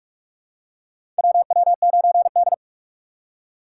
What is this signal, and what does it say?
Morse code, heard as a keyed tone, about 44 wpm, 690 Hz. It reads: WW9D
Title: USA Call Signs - 4 Characters - 2x Character Spacing - 45wpm